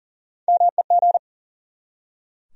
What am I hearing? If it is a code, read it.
Morse code, heard as a keyed tone, about 40 wpm, 700 Hz. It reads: MEG
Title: USA Common Names - Max 4 Letters - 40wpm